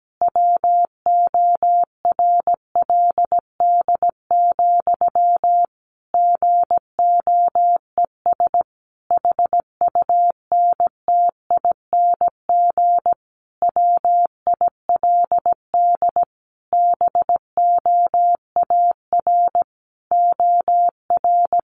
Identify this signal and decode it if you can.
Morse code, heard as a keyed tone, about 17 wpm, 700 Hz. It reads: WORLD, GOES HUNTING WILD BOAR OR
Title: A Sound Of Thunder 17wpm